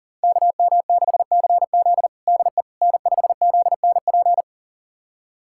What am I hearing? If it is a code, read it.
Morse code, heard as a keyed tone, about 40 wpm, 700 Hz. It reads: KM6CZ DE N5ZNP
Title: USA Callsign DE Callsign - 5 Characters - 40wpm